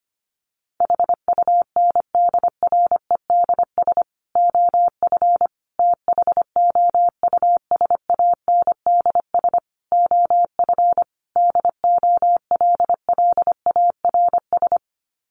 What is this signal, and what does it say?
Morse code, heard as a keyed tone, about 25 wpm, 700 Hz. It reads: HUNDREDS OF THOUSANDS OF DOLLARS
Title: Sets of 5 Words Encore 25wpm